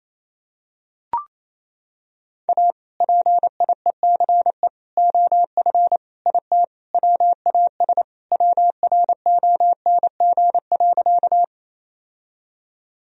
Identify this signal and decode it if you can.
Morse code, heard as a keyed tone, about 27 wpm, 700 Hz. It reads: A PIECE OF IT WAS WRONG.
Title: Sentences from Top 1000 Words - Review Part 2 - 28wpm